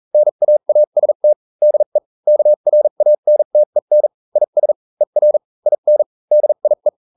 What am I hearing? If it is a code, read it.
Morse code, heard as a keyed tone, about 40 wpm, spaced as for 35 wpm, 600 Hz. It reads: NAAST DE KRANTEN IS ER IN DIE